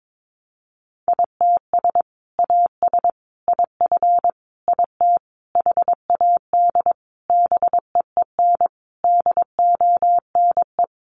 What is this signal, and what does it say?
Morse code, heard as a keyed tone, about 22 wpm, 700 Hz. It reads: ITS AS IF IT HAD BEEN DONE
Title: Sentences from Top 800 Words 22wpm